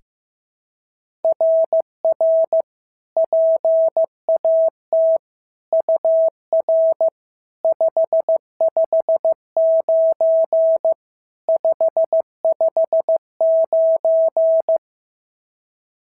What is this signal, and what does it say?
Morse code, heard as a keyed tone, about 15 wpm, 650 Hz. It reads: RR PAT UR 559 559
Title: SOTA QSO Practice 15wpm